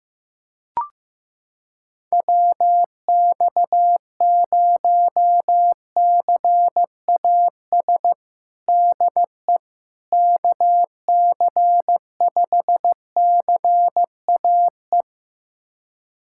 Morse code, heard as a keyed tone, about 16 wpm, 700 Hz. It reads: WX0CAS DE KC5CAE
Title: USA Callsign DE Callsign - 6 Characters - 15wpm